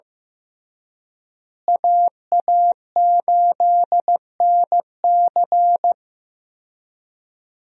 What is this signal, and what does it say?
Morse code, heard as a keyed tone, about 16 wpm, 700 Hz. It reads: AA8NC
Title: USA Call Signs - 5 Characters - Repeated 3x - 15wpm